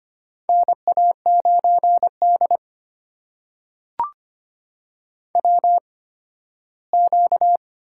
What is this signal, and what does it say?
Morse code, heard as a keyed tone, about 25 wpm, spaced as for 15 wpm, 700 Hz. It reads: NA9D WQ
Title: USA Call Signs - 8x Character Spacing - 25wpm